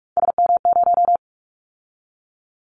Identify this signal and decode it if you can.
Morse code, heard as a keyed tone, about 46 wpm, 700 Hz. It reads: SM0
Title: Callsign Prefixes for Top 25 DXCC - 45wpm